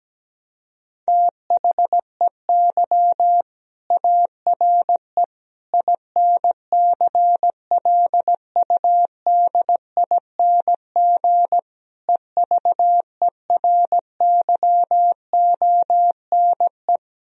Morse code, heard as a keyed tone, about 17 wpm, 700 Hz. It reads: THEY ARE INCLUDING EVERYONE
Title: Sentences from Top 1000 Words - Review Part 4 - 17wpm